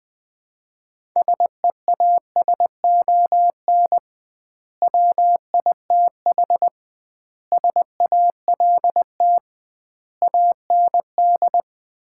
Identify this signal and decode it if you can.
Morse code, heard as a keyed tone, about 21 wpm, 700 Hz. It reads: SEASON WITH SALT AND
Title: 2x Word-Spacing - Sets of 5 Words - 20wpm